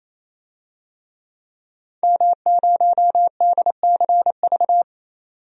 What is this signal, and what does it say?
Morse code, heard as a keyed tone, about 28 wpm, 700 Hz. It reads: M0DCV
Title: DXCC-100 Call Signs - 5 Characters - 28wpm